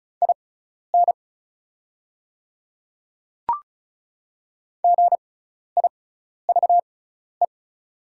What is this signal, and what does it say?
Morse code, heard as a keyed tone, about 34 wpm, spaced as for 12 wpm, 690 Hz. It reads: IN GIVE
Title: 6x Character Spacing - Top 200 Words - 35wpm